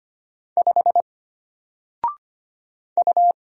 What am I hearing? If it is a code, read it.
Morse code, heard as a keyed tone, about 25 wpm, 700 Hz. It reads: 5 U
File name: Single-Letter-Number-25wpm